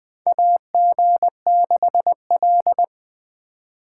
Morse code, heard as a keyed tone, about 21 wpm, 700 Hz. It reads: AG6L
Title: USA Call Signs - 4 Characters - 6x Character Spacing - 20wpm